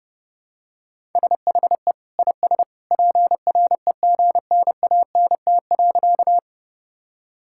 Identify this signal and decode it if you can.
Morse code, heard as a keyed tone, about 32 wpm, 700 Hz. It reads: SHE IS PREGNANT.
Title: Sentences from Top 1000 Words - Review Part 6 - 30wpm